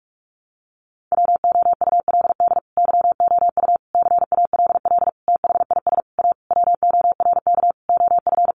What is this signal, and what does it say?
Morse code, heard as a keyed tone, about 44 wpm, 700 Hz. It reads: WOULD YOU CALL THIS A WORK OF
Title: Sentences from Top 600 Words 45wpm